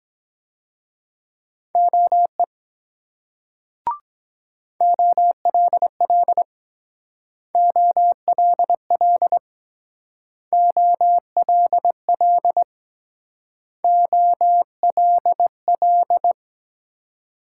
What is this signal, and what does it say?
Morse code, heard as a keyed tone, about 22 wpm, 700 Hz. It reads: OE OLL OLL OLL OLL